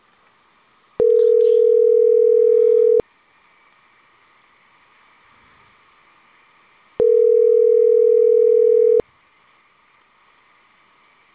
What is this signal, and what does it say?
Ringback tone (North America), 440 + 480 Hz.